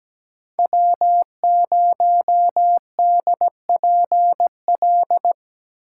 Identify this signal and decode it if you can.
Morse code, heard as a keyed tone, about 17 wpm, 700 Hz. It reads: W0DPL
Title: USA Call Signs - 5 Characters - 8x Character Spacing - 17wpm